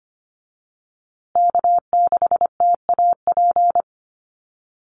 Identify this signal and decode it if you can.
Morse code, heard as a keyed tone, about 25 wpm, 700 Hz. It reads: K6TAP
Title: USA Call Signs Repeated 2x 25wpm